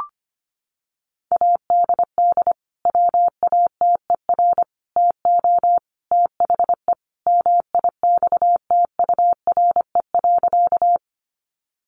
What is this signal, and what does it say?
Morse code, heard as a keyed tone, about 25 wpm, 700 Hz. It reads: ADD WATER TO THE MIXTURE.
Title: Sentences from Top 1500 Words 25wpm